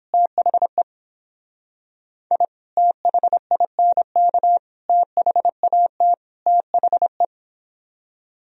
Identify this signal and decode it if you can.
Morse code, heard as a keyed tone, about 29 wpm, 700 Hz. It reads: THE I THINK THAT THE